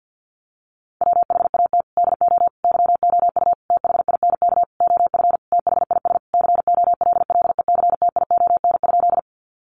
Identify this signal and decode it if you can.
Morse code, heard as a keyed tone, about 50 wpm, 700 Hz. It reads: WHAT DO YOU THINK OF THIS COLLECTION?
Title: Sentences from Top 1200 Words 50wpm